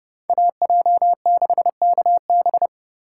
Morse code, heard as a keyed tone, about 29 wpm, 700 Hz. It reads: AJ6KB